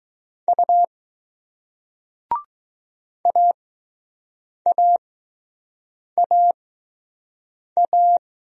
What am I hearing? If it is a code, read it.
Morse code, heard as a keyed tone, about 20 wpm, spaced as for 6 wpm, 700 Hz. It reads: U AAAA